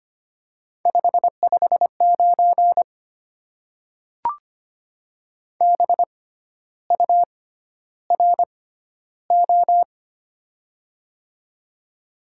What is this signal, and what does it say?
Morse code, heard as a keyed tone, about 25 wpm, spaced as for 12 wpm, 700 Hz. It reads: 559 BURO
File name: Top 90 QSO Elements - 6x Character Spacing - 25wpm